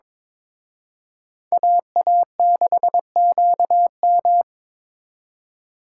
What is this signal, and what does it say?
Morse code, heard as a keyed tone, about 26 wpm, 700 Hz. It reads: AA6QM
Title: Speed Racing - USA 2x2 Call Signs 22wpm